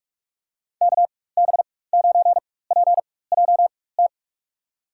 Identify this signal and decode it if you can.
Morse code, heard as a keyed tone, about 46 wpm, spaced as for 21 wpm, 700 Hz. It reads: KB9PJT
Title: USA Call Signs - 6 Characters - 4x Character Spacing - 45wpm